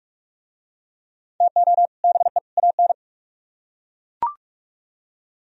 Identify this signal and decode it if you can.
Morse code, heard as a keyed tone, about 41 wpm, 700 Hz. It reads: TO BE AN